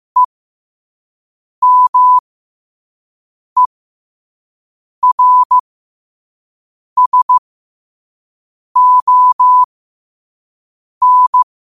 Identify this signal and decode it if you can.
Morse code, heard as a keyed tone, about 15 wpm, spaced as for 5 wpm, 1000 Hz. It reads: EMERSON
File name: qotdfiles / 5wpm